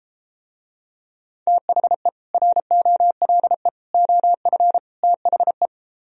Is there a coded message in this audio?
Yes: Morse code from 1.5 to 5.7 s, heard as a keyed tone, about 33 wpm, 700 Hz. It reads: THE ROLE OF THE